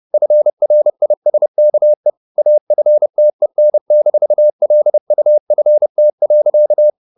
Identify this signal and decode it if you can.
Morse code, heard as a keyed tone, about 30 wpm, 600 Hz. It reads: FRISKE AFTEN-LUFT.
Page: Ferie 30 wpm